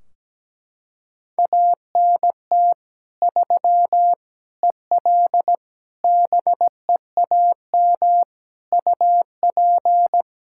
Morse code, heard as a keyed tone, about 17 wpm, 700 Hz. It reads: ANT 3 EL BEAM UP
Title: Practice QSO - Exchange 2 - Info - 17wpm